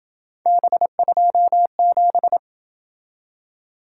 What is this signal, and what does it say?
Morse code, heard as a keyed tone, about 27 wpm, 700 Hz. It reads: B27